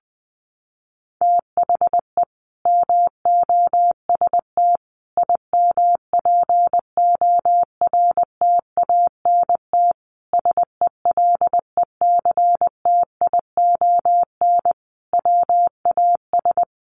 Morse code, heard as a keyed tone, about 19 wpm, 700 Hz. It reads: THE MOST IMPORTANT SELECTION WAS
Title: Sentences from Top 1400 Words 20wpm